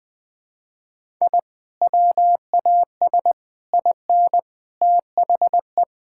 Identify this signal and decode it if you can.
Morse code, heard as a keyed tone, about 21 wpm, 700 Hz. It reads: I WAS IN THE
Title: Sentences with 3-Word Phrases - Spoken Beginning - 20wpm